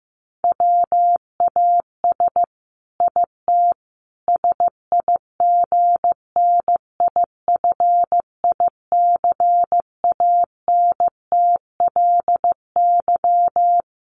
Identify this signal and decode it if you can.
Morse code, heard as a keyed tone, about 15 wpm, 700 Hz. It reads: WAS IT SIGNIFICANTLY
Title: Sentences from Top 1500 Words 15wpm